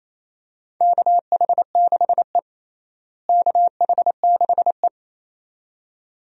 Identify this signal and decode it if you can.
Morse code, heard as a keyed tone, about 28 wpm, 700 Hz. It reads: KH6E KH6E
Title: DXCC-100 Call Signs - 4 Characters - Repeated 2x - 28wpm